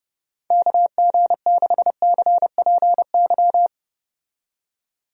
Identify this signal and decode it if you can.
Morse code, heard as a keyed tone, about 30 wpm, 700 Hz. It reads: KG6CPY
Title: USA Call Signs - 6 Characters - 30wpm